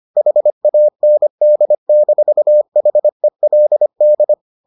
Morse code, heard as a keyed tone, about 25 wpm, 600 Hz. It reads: HAND-HELD